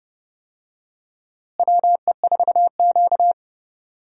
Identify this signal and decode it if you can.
Morse code, heard as a keyed tone, about 32 wpm, 700 Hz. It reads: WE4Q
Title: USA Call Signs - 4 Characters - Repeated 3x - 30wpm